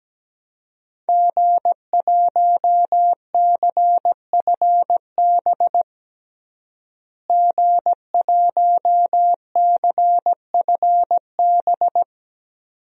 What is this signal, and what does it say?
Morse code, heard as a keyed tone, about 17 wpm, 700 Hz. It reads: G1CFB G1CFB
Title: DXCC-100 Call Signs - 5 Characters - Repeated 2x - 17wpm